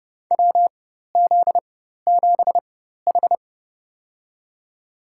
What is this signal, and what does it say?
Morse code, heard as a keyed tone, about 30 wpm, spaced as for 14 wpm, 700 Hz. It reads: WZ7H
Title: USA Call Signs - 4x Character Spacing - 30wpm